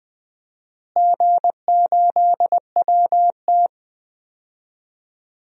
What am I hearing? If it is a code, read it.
Morse code, heard as a keyed tone, about 21 wpm, 700 Hz. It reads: G8WT
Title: DXCC-100 Call Signs - 4 Characters - Repeated 3x - 20wpm